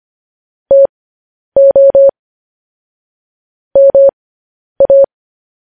Morse code, heard as a keyed tone, about 25 wpm, spaced as for 10 wpm, 550 Hz. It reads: TO MA
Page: news-headlines-in-morse-code-at-10-wpm